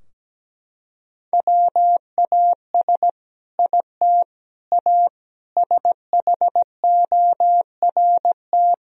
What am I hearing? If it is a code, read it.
Morse code, heard as a keyed tone, about 17 wpm, 700 Hz. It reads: WAS IT A SHORT
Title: Sentences from Top 1000 Words - Review Part 4 - 17wpm